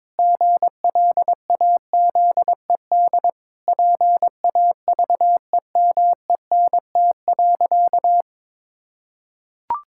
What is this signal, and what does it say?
Morse code, heard as a keyed tone, about 22 wpm, 700 Hz. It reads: GLAZED PAVEMENT.
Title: Pangram Challenge v2 - 22wpm